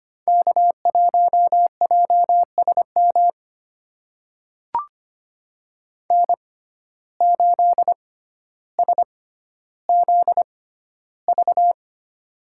Morse code, heard as a keyed tone, about 25 wpm, spaced as for 13 wpm, 700 Hz. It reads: K1JSM N8SZV
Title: USA Call Signs - 5 Characters - 6x Character Spacing - 25wpm